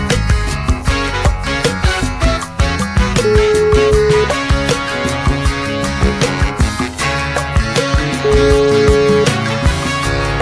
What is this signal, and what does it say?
Ringback tone (Europe, 425 Hz), with noise.